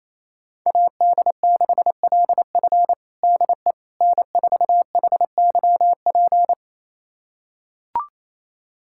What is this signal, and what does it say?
Morse code, heard as a keyed tone, about 28 wpm, 700 Hz. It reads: AD6LF DE N4HYP
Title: USA Callsign DE Callsign - 5 Characters - 28wpm